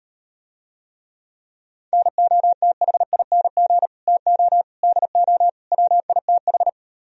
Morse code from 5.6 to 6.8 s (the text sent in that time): WITH